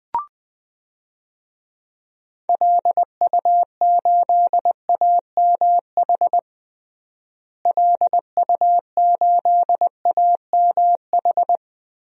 Morse code, heard as a keyed tone, about 20 wpm, 700 Hz. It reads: LU8AMH LU8AMH
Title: DXCC-100 Call Signs - 6 Characters - Repeated 2x - 20wpm